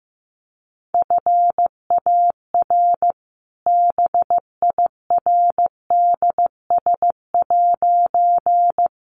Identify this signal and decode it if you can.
Morse code, heard as a keyed tone, about 15 wpm, 700 Hz. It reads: FAR BIRDS'
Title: A Sound Of Thunder 15wpm